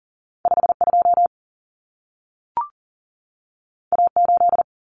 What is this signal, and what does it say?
Morse code, heard as a keyed tone, about 40 wpm, 700 Hz. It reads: 52 A8
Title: Common Cut Numbers - 2-Digit 40wpm